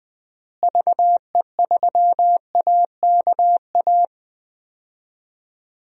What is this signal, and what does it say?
Morse code, heard as a keyed tone, about 21 wpm, 700 Hz. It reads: VE3AKA
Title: DXCC-100 Call Signs - 6 Characters - Repeated 3x - 20wpm